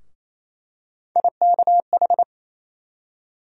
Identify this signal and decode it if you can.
Morse code, heard as a keyed tone, about 28 wpm, 700 Hz. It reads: IKH
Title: USA Call Sign Postfixes - 3 Letters 28wpm